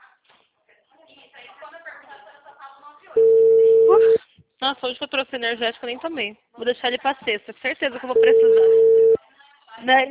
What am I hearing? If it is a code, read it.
Ringback tone (China, 450 Hz), with speech.